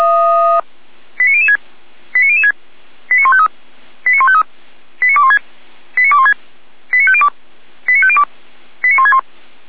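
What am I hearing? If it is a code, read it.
Five-tone selective call (ZVEI), heard as quick runs of five tones, tones of about 70 ms. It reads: 89006 89006 89153 89153 89216 89216 89502 89502 89161
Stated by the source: ZVEI five-tone selective call received on an SDR.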